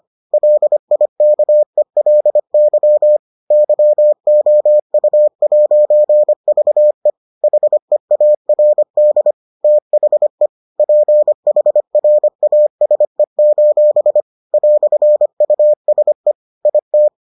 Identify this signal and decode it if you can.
Morse code, heard as a keyed tone, about 25 wpm, 600 Hz. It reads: LIKELY YOU'VE HEARD THE PHRASE: "USE IT